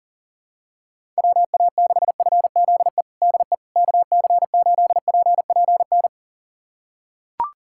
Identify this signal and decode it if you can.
Morse code, heard as a keyed tone, about 40 wpm, 700 Hz. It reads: WA6FZE DE KC8PPN
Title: USA Callsign DE Callsign - 6 Characters - 40wpm